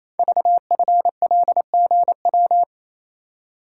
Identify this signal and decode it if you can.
Morse code, heard as a keyed tone, about 29 wpm, 700 Hz. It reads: VFLGW
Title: Code Groups - 5 Letters - 28wpm